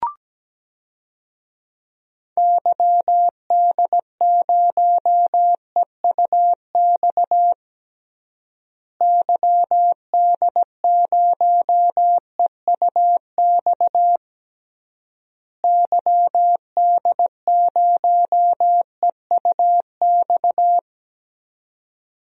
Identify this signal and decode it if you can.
Morse code, heard as a keyed tone, about 17 wpm, 700 Hz. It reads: YD0EUX YD0EUX YD0EUX
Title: DXCC-100 Call Signs - 6 Characters - Repeated 3x - 17wpm